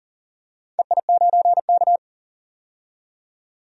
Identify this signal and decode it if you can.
Morse code, heard as a keyed tone, about 40 wpm, 700 Hz. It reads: EI9K
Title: DXCC-100 Call Signs - 4 Characters - 40wpm